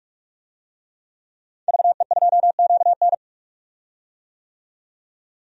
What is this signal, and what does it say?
Morse code, heard as a keyed tone, about 42 wpm, 700 Hz. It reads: VE2QN
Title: DXCC-100 Call Signs - 5 Characters - 45wpm